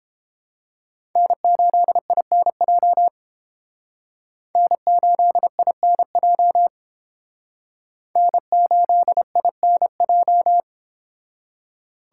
Morse code, heard as a keyed tone, about 30 wpm, 700 Hz. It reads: N8INJ N8INJ N8INJ